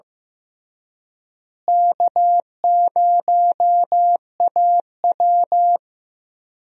Morse code, heard as a keyed tone, about 15 wpm, 700 Hz. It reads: K0AW
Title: USA Call Signs - 4 Characters - Repeated 3x - 15wpm